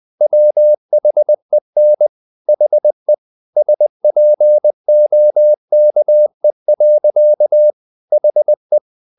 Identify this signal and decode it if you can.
Morse code, heard as a keyed tone, about 20 wpm, 600 Hz. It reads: WHEN HE SPOKE. HE
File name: the old man and the sea / the old man and the sea part 1 20wpm